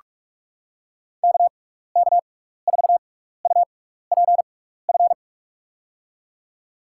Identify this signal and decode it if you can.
Morse code, heard as a keyed tone, about 46 wpm, spaced as for 16 wpm, 700 Hz. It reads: KK4UPF
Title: USA Call Signs - 6 Characters - 6x Character Spacing - 45wpm